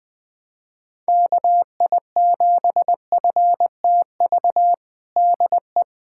Morse code, heard as a keyed tone, about 21 wpm, 700 Hz. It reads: KI7FTV DE
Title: USA Callsign DE Callsign - 6 Characters - 20wpm